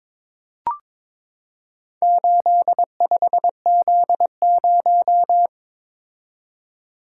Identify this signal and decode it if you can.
Morse code, heard as a keyed tone, about 22 wpm, 700 Hz. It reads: 85Z0
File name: practice / 22wpm / Four Letter-Number Combinations - 22wpm